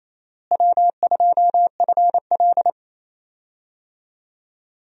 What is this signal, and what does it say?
Morse code, heard as a keyed tone, about 28 wpm, 700 Hz. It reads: W2FL
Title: USA Call Signs - 4 Characters - Repeated 2x - 28wpm